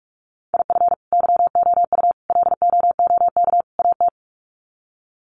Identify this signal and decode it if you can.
Morse code, heard as a keyed tone, about 44 wpm, 700 Hz. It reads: IF YOU LOOK AT
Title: Sets of 4 Words 45wpm